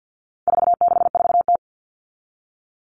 Morse code, heard as a keyed tone, about 50 wpm, 700 Hz. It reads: 464T
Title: All Cut Numbers - 4-Digit 50wpm